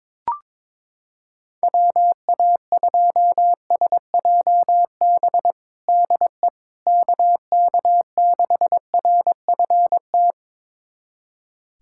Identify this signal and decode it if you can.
Morse code, heard as a keyed tone, about 22 wpm, 700 Hz. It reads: WA2SJB DE KK6RFT
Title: USA Callsign DE Callsign - 6 Characters - 22wpm